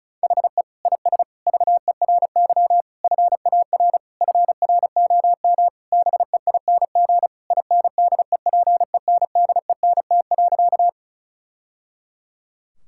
Morse code, heard as a keyed tone, about 36 wpm, 700 Hz. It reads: HE IS VERY FAR FROM BEING INDEPENDENT.
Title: Sentences from Top 1000 Words - Review Part 6 - 35wpm